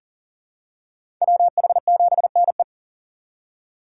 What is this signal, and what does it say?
Morse code, heard as a keyed tone, about 40 wpm, 700 Hz. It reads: WH7NE
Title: USA Call Signs - 5 Characters - 6x Character Spacing - 40wpm